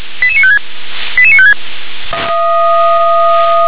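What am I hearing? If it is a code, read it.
Five-tone selective call (ZVEI), heard as quick runs of five tones, tones of about 70 ms. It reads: 88956 88956
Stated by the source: ZVEI five-tone selective call received on an SDR.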